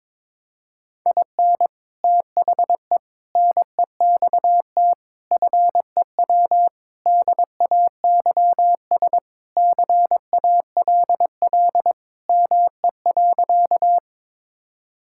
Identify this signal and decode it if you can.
Morse code, heard as a keyed tone, about 21 wpm, 700 Hz. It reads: IN THE NEXT FEW DAYS CALL ME.
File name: practice / 22wpm / Sentences with 3-Word Phrases - 22wpm